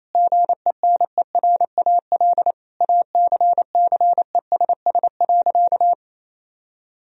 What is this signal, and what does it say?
Morse code, heard as a keyed tone, about 28 wpm, 700 Hz. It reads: GENERAL ACCESS.
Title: Sentences from Top 1000 Words - Review Part 4 - 28wpm